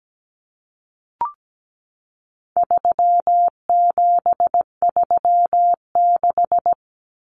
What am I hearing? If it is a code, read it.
Morse code, heard as a keyed tone, about 17 wpm, 700 Hz. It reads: 3736
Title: Numbers - 4-Digit 17wpm